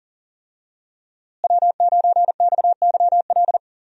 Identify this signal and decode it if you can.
Morse code, heard as a keyed tone, about 40 wpm, 700 Hz. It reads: W9XYL
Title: USA Call Signs - 5 Characters - Repeated 2x - 40wpm